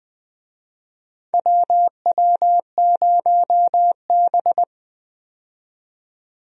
Morse code, heard as a keyed tone, about 20 wpm, 700 Hz. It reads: WW0B
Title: USA Call Signs - 4 Characters - 20wpm